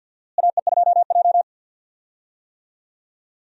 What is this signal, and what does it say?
Morse code, heard as a keyed tone, about 49 wpm, 700 Hz. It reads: AE2J
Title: USA Call Signs - 4 Characters - Repeated 3x - 50wpm